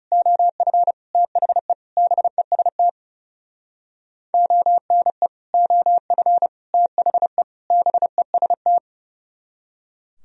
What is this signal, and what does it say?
Morse code, heard as a keyed tone, about 38 wpm, 700 Hz. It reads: OF THE BEST ONE OF THE BEST